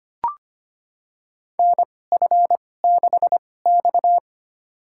Morse code, heard as a keyed tone, about 25 wpm, spaced as for 18 wpm, 700 Hz. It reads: NF6X
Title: USA Call Signs - 4 Characters - 2x Character Spacing - 25wpm